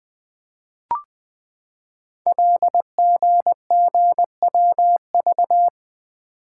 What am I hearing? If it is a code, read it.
Morse code, heard as a keyed tone, about 20 wpm, 700 Hz. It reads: LGGWV
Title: Code Groups - 5 Letters - 20wpm